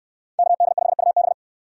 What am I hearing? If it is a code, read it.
Morse code, heard as a keyed tone, about 69 wpm, 700 Hz. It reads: KD5UB